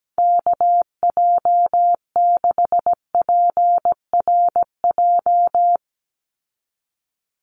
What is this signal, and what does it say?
Morse code, heard as a keyed tone, about 17 wpm, 700 Hz. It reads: KJ6PRJ
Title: USA Call Signs 17wpm